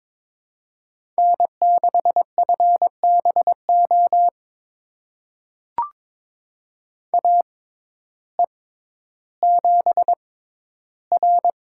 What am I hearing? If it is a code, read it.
Morse code, heard as a keyed tone, about 22 wpm, spaced as for 12 wpm, 700 Hz. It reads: N6FBO AE7R
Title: USA Call Signs - 6x Character Spacing - 22wpm